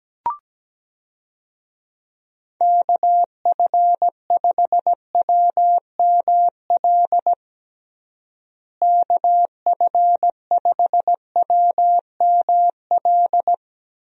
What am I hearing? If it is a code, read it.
Morse code, heard as a keyed tone, about 17 wpm, 700 Hz. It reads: KF5WML KF5WML
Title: USA Call Signs - 6 Characters - Repeated 3x - 17wpm